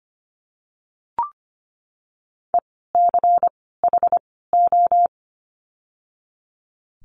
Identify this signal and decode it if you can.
Morse code, heard as a keyed tone, about 25 wpm, spaced as for 16 wpm, 700 Hz. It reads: ECHO
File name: course / 25:10wpm / 094 Lesson 16f - Letter c - Words 25wpm